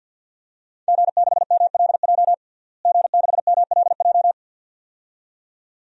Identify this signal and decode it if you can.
Morse code, heard as a keyed tone, about 50 wpm, 700 Hz. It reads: G6MLJ G6MLJ
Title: DXCC-100 Call Signs - 5 Characters - Repeated 2x - 50wpm